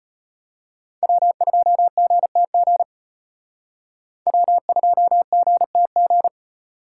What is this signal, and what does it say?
Morse code, heard as a keyed tone, about 31 wpm, 700 Hz. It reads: W2GTG W2GTG